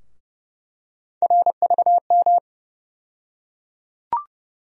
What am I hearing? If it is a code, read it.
Morse code, heard as a keyed tone, about 30 wpm, 700 Hz. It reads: RVM
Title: USA Call Sign Postfixes - 3 Letters 30wpm